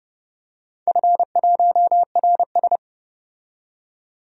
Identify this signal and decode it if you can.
Morse code, heard as a keyed tone, about 30 wpm, 700 Hz. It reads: F1RS
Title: DXCC-100 Call Signs - 4 Characters - 30wpm